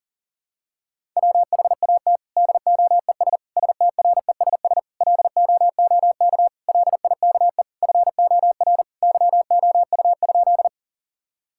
Morse code, heard as a keyed tone, about 41 wpm, 700 Hz. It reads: WHAT DOES STRESS LOOK LIKE FOR YOU?
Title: Sentences from Top 1000 Words - Review Part 6 - 40wpm